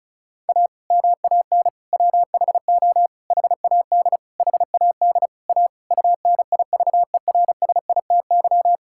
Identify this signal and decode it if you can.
Morse code, heard as a keyed tone, about 34 wpm, 700 Hz. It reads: A MAN WHO HAD HAD A UNIVERSITY
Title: Sentences from Top 1000 Words - Review Part 6 - 35wpm